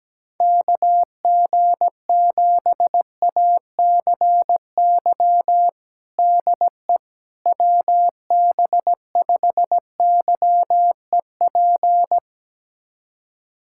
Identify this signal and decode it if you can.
Morse code, heard as a keyed tone, about 17 wpm, 700 Hz. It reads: KG7ACY DE WB5YEP
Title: USA Callsign DE Callsign - 6 Characters - 17wpm